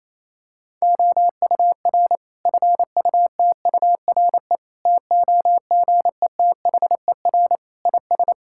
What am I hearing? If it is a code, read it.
Morse code, heard as a keyed tone, about 28 wpm, 700 Hz. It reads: OUR FUTURE TOGETHER IS
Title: Sentences from Top 1000 Words - Review Part 5 - 28wpm